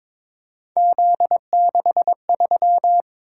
Morse code, heard as a keyed tone, about 22 wpm, 700 Hz. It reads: Z63
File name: practice / 22wpm / Three Letter-Number Combinations - 22wpm